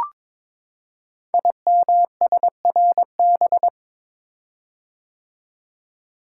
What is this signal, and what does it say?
Morse code, heard as a keyed tone, about 22 wpm, 700 Hz. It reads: IMSRB
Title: Code Groups - 5 Letters - 22wpm